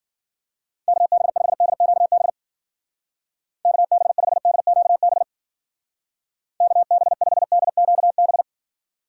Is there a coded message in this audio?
Yes: Morse code from 0.9 to 8.4 s, heard as a keyed tone, about 52 wpm, 700 Hz. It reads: KB5DQB KB5DQB KB5DQB